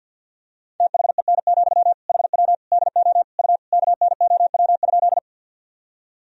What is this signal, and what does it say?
Morse code, heard as a keyed tone, about 47 wpm, 700 Hz. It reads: THEN, HW DO U KNOW?
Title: Sentences from Top 100 Words and Abbreviations 50wpm